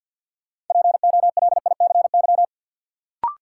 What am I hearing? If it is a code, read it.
Morse code, heard as a keyed tone, about 46 wpm, 700 Hz. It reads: POLICY
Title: Sentences from Top 1200 Words - New Word Drill Alt - 50wpm